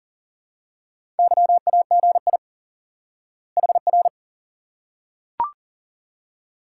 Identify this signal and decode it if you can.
Morse code, heard as a keyed tone, about 40 wpm, 700 Hz. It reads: YAGI HR
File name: IQE Course 3 - Lesson 89 - YAGI - Two Elements - 40wpm